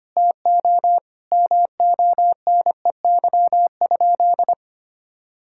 Morse code, heard as a keyed tone, about 25 wpm, 700 Hz. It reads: TO MONEY?
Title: Sentences from Top 1000 Words - Review Part 6 - 25wpm